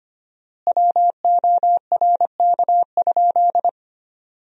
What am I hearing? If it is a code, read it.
Morse code, heard as a keyed tone, about 25 wpm, 700 Hz. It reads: WORK?
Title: 4x Word-Spacing - Sentences from Top 500 Words - 25wpm